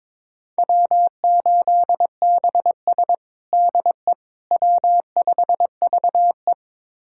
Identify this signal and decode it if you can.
Morse code, heard as a keyed tone, about 22 wpm, 700 Hz. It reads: W8BS DE W5VE
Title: USA Callsign DE Callsign - 4 Characters - 22wpm